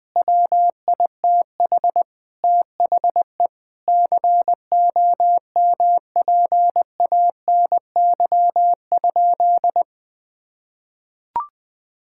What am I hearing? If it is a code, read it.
Morse code, heard as a keyed tone, about 20 wpm, 700 Hz. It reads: WITH THE COMPANY?
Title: Sentences from Top 1600 Words 20wpm